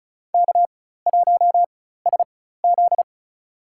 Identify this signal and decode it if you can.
Morse code, heard as a keyed tone, about 34 wpm, spaced as for 16 wpm, 700 Hz. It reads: K1SZ
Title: USA Call Signs - 4 Characters - 4x Character Spacing - 35wpm